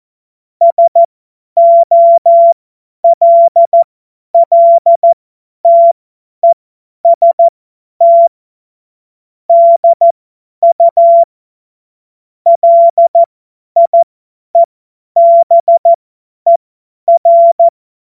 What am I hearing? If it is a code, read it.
Morse code, heard as a keyed tone, about 14 wpm, spaced as for 10 wpm, 680 Hz. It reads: SOLLTEST DU LIEBER
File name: Morsekurs / Witze / Witz 27. September 2022 50 F70